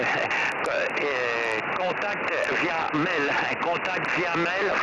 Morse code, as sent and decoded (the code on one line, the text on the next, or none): none